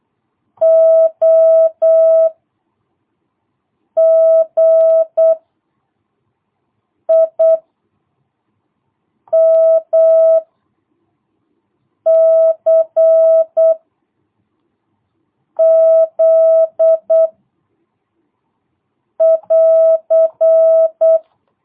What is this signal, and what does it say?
Morse code, heard as a keyed tone, about 8 wpm, 660 Hz. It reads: O G I M C Z <AR>
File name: jh8cbh / 2022cw